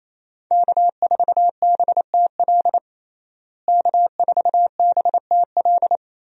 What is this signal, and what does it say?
Morse code, heard as a keyed tone, about 27 wpm, 700 Hz. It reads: K4BTL K4BTL
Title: USA Call Signs - 5 Characters - Repeated 3x - 28wpm